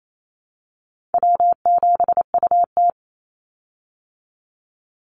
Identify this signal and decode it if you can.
Morse code, heard as a keyed tone, about 28 wpm, 700 Hz. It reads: W7UT
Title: Call Signs Repeated 2x - 28wpm